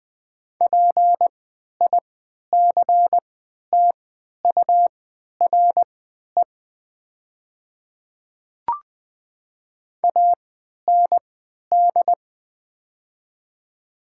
Morse code, heard as a keyed tone, about 20 wpm, spaced as for 11 wpm, 700 Hz. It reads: PICTURE AND
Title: 3x Character Spacing - Top 300 Words - 20wpm